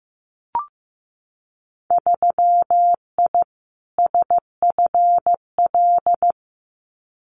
Morse code, heard as a keyed tone, about 15 wpm, 700 Hz. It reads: 3I SFL
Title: Winter Field Day Exchanges 15wpm